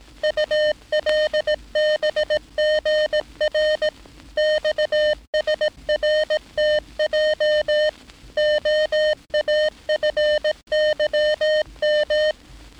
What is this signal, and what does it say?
Morse code, heard as a keyed tone, about 18 wpm, 610 Hz. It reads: ULBGR XSRTJ OAFYM
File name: training / NAVPERS 11049to11060 / NAVPERS 11059 RA RB